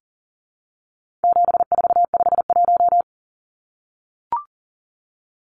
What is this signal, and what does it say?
Morse code, heard as a keyed tone, about 40 wpm, 700 Hz. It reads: 7451
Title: Numbers - 4-Digit 40wpm